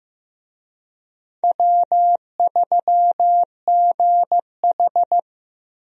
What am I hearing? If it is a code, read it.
Morse code, heard as a keyed tone, about 16 wpm, 700 Hz. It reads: W3GH
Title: USA Call Signs - 4 Characters - 15wpm